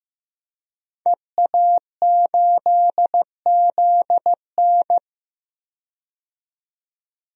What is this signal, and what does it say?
Morse code, heard as a keyed tone, about 15 wpm, 700 Hz. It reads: EA8ZN
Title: DXCC-100 Call Signs - 5 Characters - 15wpm